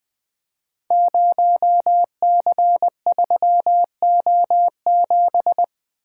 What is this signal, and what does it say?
Morse code, heard as a keyed tone, about 20 wpm, 700 Hz. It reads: 0C3O7
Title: Code Groups - 5 Letter:Numbers - 20wpm